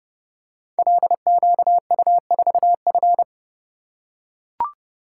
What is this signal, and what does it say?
Morse code, heard as a keyed tone, about 31 wpm, 700 Hz. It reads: LQU4F
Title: Code Groups - 5 Letter:Numbers - 30wpm